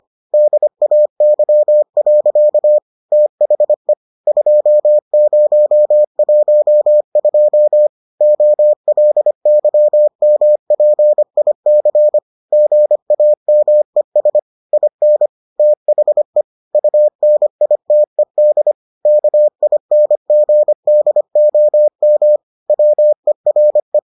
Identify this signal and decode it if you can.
Morse code, heard as a keyed tone, about 25 wpm, 600 Hz. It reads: DAY. THE 2012 OLYMPIC GAMES IN THE UNITED KINGDOM WERE